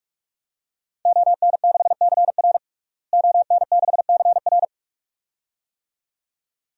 Morse code, heard as a keyed tone, about 46 wpm, 700 Hz. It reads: ON6CR ON6CR
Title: DXCC-100 Call Signs - 5 Characters - Repeated 2x - 45wpm